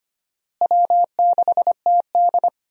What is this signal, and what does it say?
Morse code, heard as a keyed tone, about 25 wpm, 700 Hz. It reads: W6TD
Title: USA Call Signs - 4 Characters - Repeated 2x - 25wpm